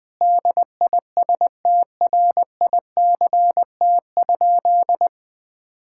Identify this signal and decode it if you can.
Morse code, heard as a keyed tone, about 20 wpm, 700 Hz. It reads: DISTRICT?
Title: Sentences from Top 1600 Words 20wpm